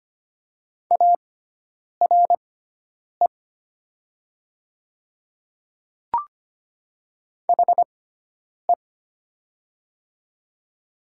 Morse code, heard as a keyed tone, about 25 wpm, spaced as for 9 wpm, 700 Hz. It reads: ARE HE
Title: 6x Character Spacing - Top 100 Words - 25wpm